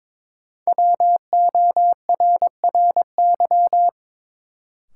Morse code, heard as a keyed tone, about 22 wpm, 700 Hz. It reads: WORRY
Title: Sentences from Top 1100 Words - New Word Drill - 22wpm